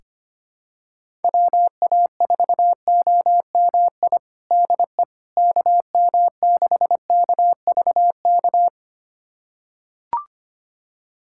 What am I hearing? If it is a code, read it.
Morse code, heard as a keyed tone, about 25 wpm, 700 Hz. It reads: WA4OMI DE KM6KVK
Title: USA Callsign DE Callsign - 6 Characters - 25wpm